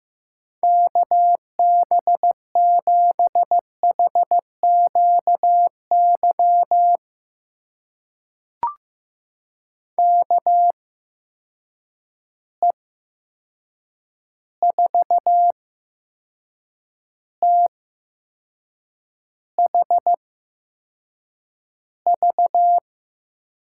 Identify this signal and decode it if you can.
Morse code, heard as a keyed tone, about 15 wpm, spaced as for 6 wpm, 700 Hz. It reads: KB7HQY KE4THV
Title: USA Call Signs - 6 Characters - 8x Character Spacing - 15wpm